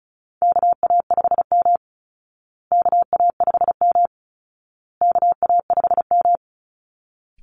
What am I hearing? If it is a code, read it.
Morse code, heard as a keyed tone, about 34 wpm, 700 Hz. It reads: KA5M KA5M KA5M
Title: Call Signs Repeated 2x - 35wpm